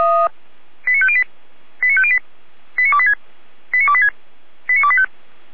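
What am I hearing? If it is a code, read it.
Five-tone selective call (ZVEI), heard as quick runs of five tones, tones of about 70 ms. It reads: 89508 89508 89286 89286 89285
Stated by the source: ZVEI five-tone selective call received on an SDR.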